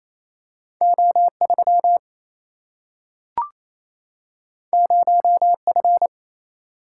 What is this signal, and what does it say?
Morse code, heard as a keyed tone, about 28 wpm, 700 Hz. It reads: O 3 0 F
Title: Single-Letter Single-Number 28wpm